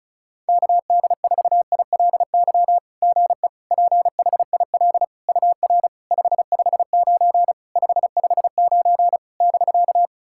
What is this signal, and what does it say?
Morse code, heard as a keyed tone, about 36 wpm, 700 Hz. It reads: KD4ILY GE PHIL UR 559 559 <BK>
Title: SOTA QSO Practice 35wpm